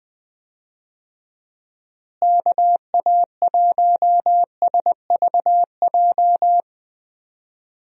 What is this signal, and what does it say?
Morse code, heard as a keyed tone, about 21 wpm, 700 Hz. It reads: KA1SVJ
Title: USA Call Signs - 6 Characters - Repeated 2x - 20wpm